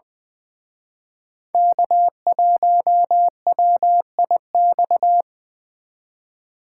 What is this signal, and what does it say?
Morse code, heard as a keyed tone, about 23 wpm, 700 Hz. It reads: K1WIX
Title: Speed Racing - USA Call Signs 20wpm